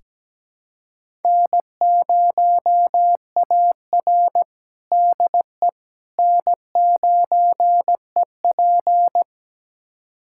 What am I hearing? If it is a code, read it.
Morse code, heard as a keyed tone, about 17 wpm, 700 Hz. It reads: N0AR DE N9EP
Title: USA Callsign DE Callsign - 4 Characters - 17wpm